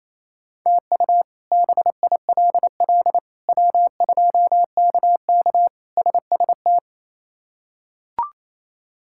Morse code, heard as a keyed tone, about 27 wpm, 700 Hz. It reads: TU BILL W2KK SST
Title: K1USN SST QSO Practice - 28wpm